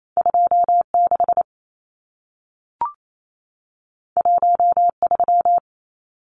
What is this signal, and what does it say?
Morse code, heard as a keyed tone, about 28 wpm, 700 Hz. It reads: 26 13
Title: Numbers - 2-Digit 28wpm